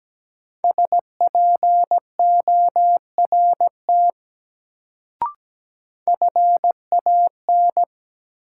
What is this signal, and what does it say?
Morse code, heard as a keyed tone, about 17 wpm, 700 Hz. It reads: SPORT FAN
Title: Sentences from Top 1000 Words - Review Part 4 - New Word Drill - 17wpm